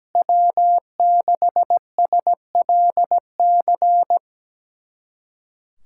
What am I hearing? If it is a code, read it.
Morse code, heard as a keyed tone, about 17 wpm, 700 Hz. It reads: W6SLC
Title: USA Callsign DE Callsign 17wpm